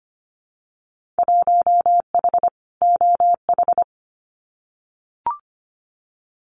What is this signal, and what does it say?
Morse code, heard as a keyed tone, about 25 wpm, 700 Hz. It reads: 1H OH
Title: Winter Field Day Exchanges 25wpm